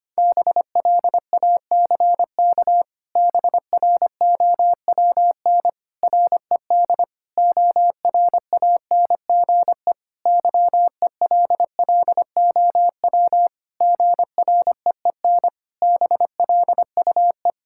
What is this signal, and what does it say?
Morse code, heard as a keyed tone, about 25 wpm, 700 Hz. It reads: BLACK BROWN RED ORANGE YELLOW GREEN BLUE
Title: Pangram Challenge v2 - 25wpm